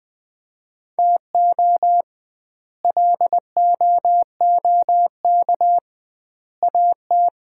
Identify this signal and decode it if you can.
Morse code, heard as a keyed tone, about 20 wpm, 700 Hz. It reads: TO LOOK AT
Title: 2x Word-Spacing - Sets of 3 Words - 20wpm